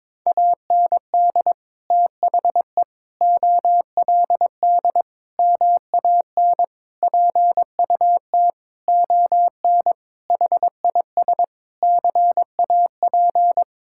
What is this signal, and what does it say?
Morse code, heard as a keyed tone, about 22 wpm, 700 Hz. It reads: AND THE OLD MAN PUT ON HIS CAP
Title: Sentences from Top 1600 Words 22wpm